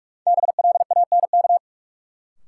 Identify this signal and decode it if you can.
Morse code, heard as a keyed tone, about 46 wpm, 700 Hz. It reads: BLANK
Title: Sentences from Top 1500 Words - New Word Drill - 45wpm